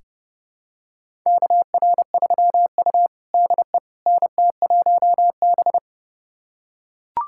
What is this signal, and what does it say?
Morse code, heard as a keyed tone, about 30 wpm, 700 Hz. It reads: KR3U DE NT1B
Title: USA Callsign DE Callsign - 4 Characters - 30wpm